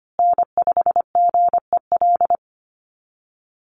Morse code, heard as a keyed tone, about 25 wpm, 700 Hz. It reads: N5GEL
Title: USA Call Signs Repeated 2x 25wpm